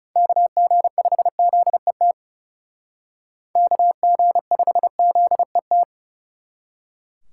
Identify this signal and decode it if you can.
Morse code, heard as a keyed tone, about 37 wpm, 700 Hz. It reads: KG5ZET KG5ZET
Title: Speed Racing - USA 2x3 Call Signs 30wpm